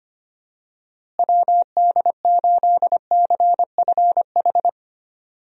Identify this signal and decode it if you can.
Morse code, heard as a keyed tone, about 25 wpm, 700 Hz. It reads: WD8CFH
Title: USA Call Signs - 6 Characters - Repeated 2x - 25wpm